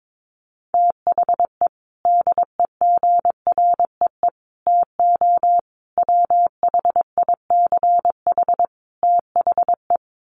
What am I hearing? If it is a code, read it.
Morse code, heard as a keyed tone, about 22 wpm, 700 Hz. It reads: THE DEGREE TO WHICH THE
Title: Sets of 5 Words Encore2 22wpm